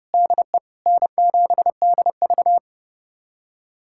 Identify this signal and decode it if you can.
Morse code, heard as a keyed tone, about 30 wpm, 700 Hz. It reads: DE N7DV
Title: USA Callsign DE Callsign - 4 Characters - 30wpm